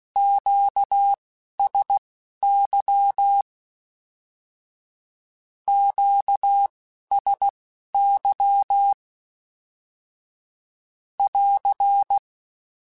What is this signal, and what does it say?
Morse code, heard as a keyed tone, about 16 wpm, spaced as for 11 wpm, 800 Hz. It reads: QSY QSY <AR>
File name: Q Calls / Q Calls-mp3s / Q Calls Twice @10